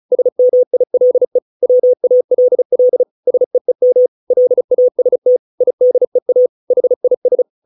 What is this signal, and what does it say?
Morse code, heard as a keyed tone, about 34 wpm, 500 Hz. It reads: SMILE WALL SEEM LAST IDEA HIS